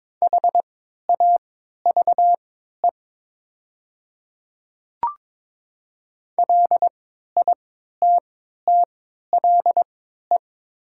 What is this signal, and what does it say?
Morse code, heard as a keyed tone, about 22 wpm, spaced as for 12 wpm, 700 Hz. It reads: HAVE LITTLE
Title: 3x Character Spacing - Top 300 Words - 22wpm